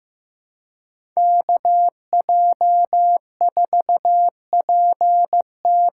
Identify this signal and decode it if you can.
Morse code, heard as a keyed tone, about 15 wpm, 700 Hz. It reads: KJ4PT
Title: USA Call Signs - 6 Characters - 6x Character Spacing - 15wpm